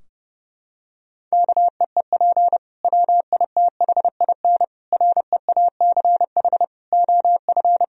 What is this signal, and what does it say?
Morse code, heard as a keyed tone, about 30 wpm, 700 Hz. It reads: KEEP WITHIN REACH OF
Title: Sentences from Top 1000 Words - Review Part 4 - 30wpm